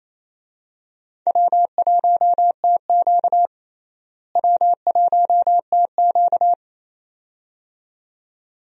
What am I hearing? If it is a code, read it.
Morse code, heard as a keyed tone, about 28 wpm, 700 Hz. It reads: W1TQ W1TQ
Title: USA Call Signs - 4 Characters - Repeated 2x - 28wpm